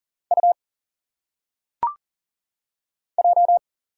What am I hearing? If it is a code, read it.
Morse code, heard as a keyed tone, about 40 wpm, 700 Hz. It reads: U J
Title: Single Letters - Phonetic Pronunciation 40wpm